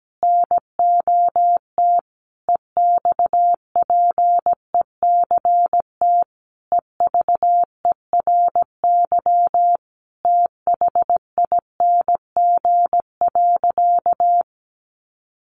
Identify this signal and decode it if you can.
Morse code, heard as a keyed tone, about 17 wpm, 700 Hz. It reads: NOT EXPECT EVERYTHING.
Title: Sentences from Top 1100 Words 17wpm